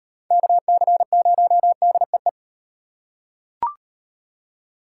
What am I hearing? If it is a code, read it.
Morse code, heard as a keyed tone, about 35 wpm, 700 Hz. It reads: KC0DEE